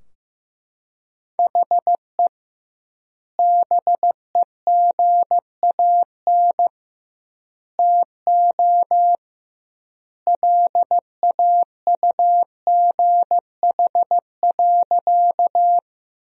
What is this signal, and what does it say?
Morse code, heard as a keyed tone, about 16 wpm, 700 Hz. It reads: HE BEGAN TO LAUGH.
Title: 2x Word-Spacing - Sentences from Top 500 Words - 15wpm